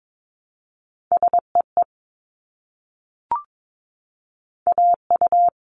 Morse code, heard as a keyed tone, about 22 wpm, 700 Hz. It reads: SEE AU
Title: Multi-letter Phonemes Repeated 3x 22wpm